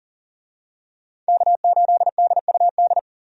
Morse code, heard as a keyed tone, about 40 wpm, 700 Hz. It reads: K8DUD
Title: USA Call Signs - 5 Characters - 40wpm